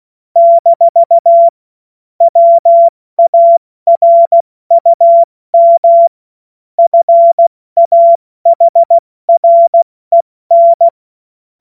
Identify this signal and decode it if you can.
Morse code, heard as a keyed tone, about 16 wpm, 680 Hz. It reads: - WARUM FAHREN
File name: Morsekurs / Witze / Witz 27 Februar 2024 70 F80